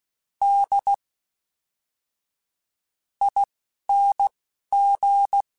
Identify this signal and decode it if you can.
Morse code, heard as a keyed tone, about 16 wpm, spaced as for 7 wpm, 800 Hz. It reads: DING
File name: Special INGs / Special INGs-mp3s / Fast ing @10